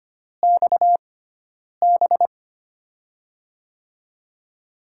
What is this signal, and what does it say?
Morse code, heard as a keyed tone, about 25 wpm, spaced as for 9 wpm, 700 Hz. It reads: XB
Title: USA Call Signs - 5 Characters - 6x Character Spacing - 25wpm